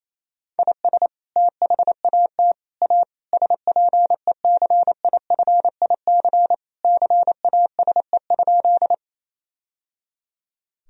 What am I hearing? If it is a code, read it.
Morse code, heard as a keyed tone, about 28 wpm, 700 Hz. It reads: IS THAT A SPECIFIC CASE?
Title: Sentences from Top 1000 Words - Review Part 3 - 28wpm